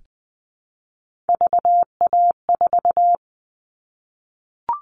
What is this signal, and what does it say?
Morse code, heard as a keyed tone, about 20 wpm, 700 Hz. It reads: VA4
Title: All Cut Numbers - 3-Digit 20wpm